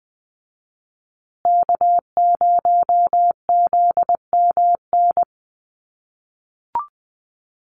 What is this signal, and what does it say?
Morse code, heard as a keyed tone, about 20 wpm, 700 Hz. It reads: K0ZMN
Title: USA Call Signs Repeated 2x 20wpm